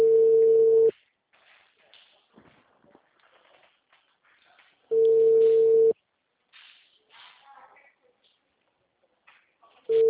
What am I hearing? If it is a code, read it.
Ringback tone (China, 450 Hz), with speech.